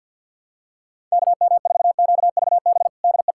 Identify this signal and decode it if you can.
Morse code, heard as a keyed tone, about 46 wpm, 700 Hz. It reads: KM4QVB DE
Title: USA Callsign DE Callsign - 6 Characters - 50wpm